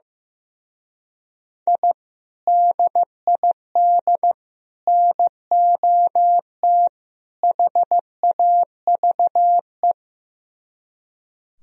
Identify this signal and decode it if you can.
Morse code, heard as a keyed tone, about 15 wpm, 700 Hz. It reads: I DID NOT HAVE
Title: Speed-Racing - Sets of 4 Words 15wpm